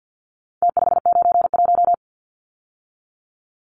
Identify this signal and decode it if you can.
Morse code, heard as a keyed tone, about 48 wpm, 700 Hz. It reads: T591
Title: Numbers - Contest Style - 4-Digit 50wpm